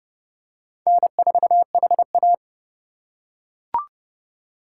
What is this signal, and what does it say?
Morse code, heard as a keyed tone, about 25 wpm, 700 Hz. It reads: N4HA